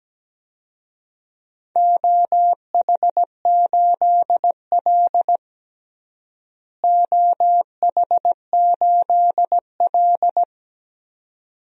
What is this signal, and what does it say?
Morse code, heard as a keyed tone, about 17 wpm, 700 Hz. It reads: OH8L OH8L
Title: DXCC-100 Call Signs - 4 Characters - Repeated 2x - 17wpm